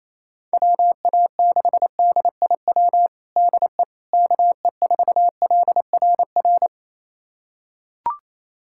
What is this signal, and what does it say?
Morse code, heard as a keyed tone, about 28 wpm, 700 Hz. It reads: WA6DIW DE KE4LRR
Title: USA Callsign DE Callsign 28wpm